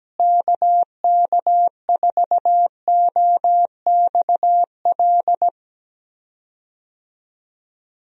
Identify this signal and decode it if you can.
Morse code, heard as a keyed tone, about 17 wpm, 700 Hz. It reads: KK4OXL
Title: USA Call Signs - 6 Characters - Repeated 2x - 17wpm